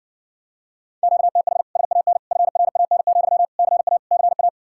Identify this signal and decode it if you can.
Morse code, heard as a keyed tone, about 60 wpm, 700 Hz. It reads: QTH SAN FRAN, CA CA